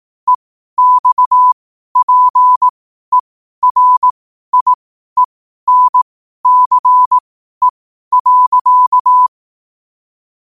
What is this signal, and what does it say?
Morse code, heard as a keyed tone, about 18 wpm, spaced as for 12 wpm, 1000 Hz. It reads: EXPERIENCE.
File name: qotdfiles / 12wpm